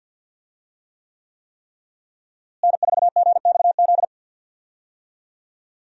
Morse code, heard as a keyed tone, about 52 wpm, 700 Hz. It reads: N4GXZ
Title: USA Call Signs - 5 Characters - Repeated 3x - 50wpm